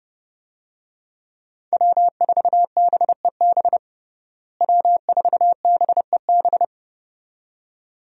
Morse code, heard as a keyed tone, about 29 wpm, 700 Hz. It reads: W4BEB W4BEB
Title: USA Call Signs - 5 Characters - Repeated 2x - 30wpm